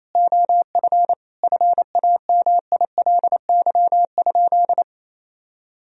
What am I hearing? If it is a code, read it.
Morse code, heard as a keyed tone, about 28 wpm, 700 Hz. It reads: OF FAMILY?
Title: Sentences from Top 1000 Words - Review Part 4 - 28wpm